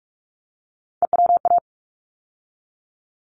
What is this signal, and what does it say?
Morse code, heard as a keyed tone, about 44 wpm, 700 Hz. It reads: EWA
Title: ARRL Sections 45wpm